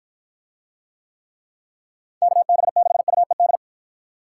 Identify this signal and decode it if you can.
Morse code, heard as a keyed tone, about 49 wpm, 700 Hz. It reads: KB6UED